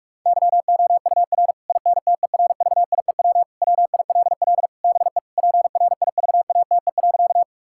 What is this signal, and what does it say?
Morse code, heard as a keyed tone, about 41 wpm, 700 Hz. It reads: YOUR INTERVIEW WILL BE PRIVATE.
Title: Sentences from Top 1600 Words 45wpm